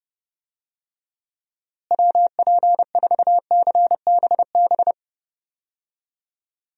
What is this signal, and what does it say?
Morse code, heard as a keyed tone, about 31 wpm, 700 Hz. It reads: WP4CBB
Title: USA Call Signs - 6 Characters - Repeated 3x - 30wpm